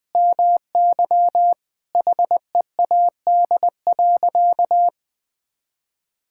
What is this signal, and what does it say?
Morse code, heard as a keyed tone, about 21 wpm, 700 Hz. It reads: MY HEAD.
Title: Sentences from Top 1600 Words 20wpm